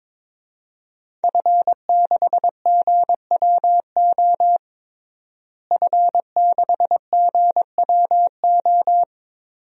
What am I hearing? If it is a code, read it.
Morse code, heard as a keyed tone, about 22 wpm, 700 Hz. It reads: F6GWO F6GWO
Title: DXCC-100 Call Signs - 5 Characters - Repeated 3x - 22wpm